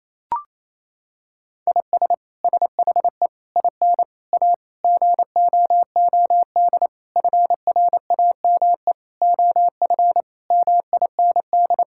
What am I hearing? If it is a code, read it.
Morse code, heard as a keyed tone, about 28 wpm, 700 Hz. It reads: IS SHE IN A GOOD FRAME OF MIND
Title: Sentences from Top 1000 Words - Review Part 6 - 28wpm